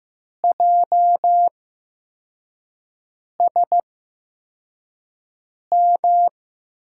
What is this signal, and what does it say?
Morse code, heard as a keyed tone, about 15 wpm, spaced as for 4 wpm, 700 Hz. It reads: JSM
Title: USA Call Signs - 5 Characters - 8x Character Spacing - 15wpm